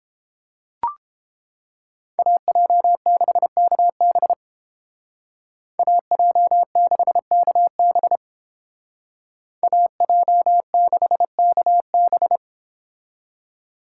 Morse code, heard as a keyed tone, about 30 wpm, 700 Hz. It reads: AJ6KB AJ6KB AJ6KB